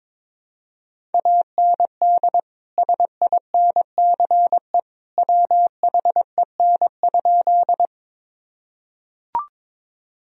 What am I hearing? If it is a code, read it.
Morse code, heard as a keyed tone, about 22 wpm, 700 Hz. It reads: AND SINCE WHEN?
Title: Sentences from Top 1000 Words - Review Part 1 - 22wpm